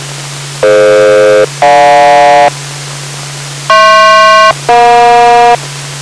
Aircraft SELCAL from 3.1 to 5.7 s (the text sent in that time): HP-EJ